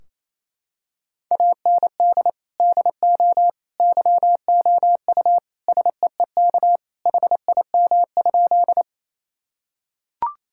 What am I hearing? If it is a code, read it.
Morse code, heard as a keyed tone, about 28 wpm, 700 Hz. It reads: AND DO YOU SEEK HIM?
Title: Sentences from Top 1600 Words 28wpm